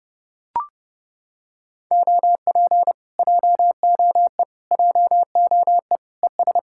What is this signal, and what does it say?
Morse code, heard as a keyed tone, about 29 wpm, 700 Hz. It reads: OP JOE JOE ES